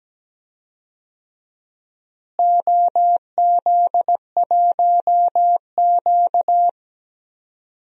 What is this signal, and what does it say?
Morse code, heard as a keyed tone, about 17 wpm, 700 Hz. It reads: OZ1Q
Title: DXCC-100 Call Signs - 4 Characters - Repeated 3x - 17wpm